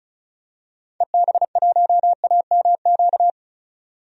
Morse code, heard as a keyed tone, about 34 wpm, 700 Hz. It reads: EB1AMQ
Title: DXCC-100 Call Signs - 6 Characters - 35wpm